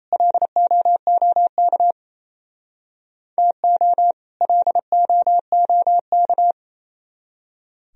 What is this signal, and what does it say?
Morse code, heard as a keyed tone, about 34 wpm, 700 Hz. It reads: LOOK TO LOOK
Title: Speed-Racing - Sets of 2 Words 28wpm